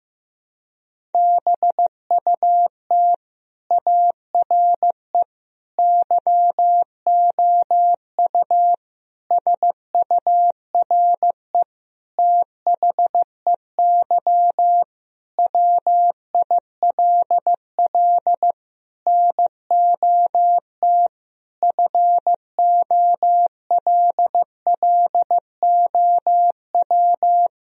Morse code, heard as a keyed tone, about 15 wpm, 700 Hz. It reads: BUT ARE YOU SURE THEY WILL NOT FOLLOW
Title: Sentences from Top 1000 Words - Review Part 1 - 15wpm